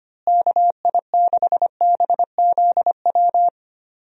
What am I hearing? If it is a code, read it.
Morse code, heard as a keyed tone, about 25 wpm, 700 Hz. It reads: KI6BZW
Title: USA Call Signs - 6 Characters - Repeated 3x - 25wpm